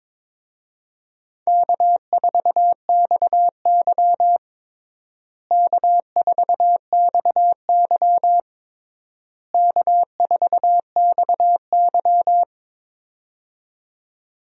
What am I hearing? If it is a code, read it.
Morse code, heard as a keyed tone, about 22 wpm, 700 Hz. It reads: K4XY K4XY K4XY
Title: USA Call Signs - 4 Characters - Repeated 3x - 22wpm